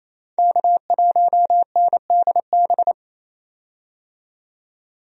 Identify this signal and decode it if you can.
Morse code, heard as a keyed tone, about 28 wpm, 700 Hz. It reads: K1NDB
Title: USA Call Signs - 5 Characters - 28wpm